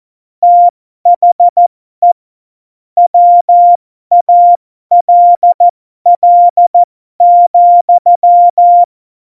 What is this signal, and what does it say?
Morse code, heard as a keyed tone, about 14 wpm, spaced as for 12 wpm, 700 Hz. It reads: THE WALL,
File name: Morsekurs / Witze